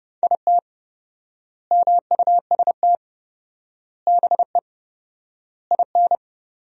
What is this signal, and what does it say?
Morse code, heard as a keyed tone, about 30 wpm, 700 Hz. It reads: IT MUST BE IN
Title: 4x Word-Spacing - Sentences from Top 300 Words - 30wpm